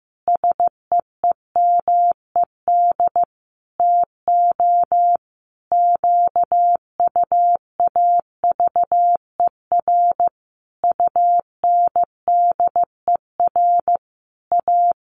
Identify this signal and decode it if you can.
Morse code, heard as a keyed tone, about 15 wpm, 700 Hz. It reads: SEEMED TO QUAVER UNDER A
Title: A Sound Of Thunder 15wpm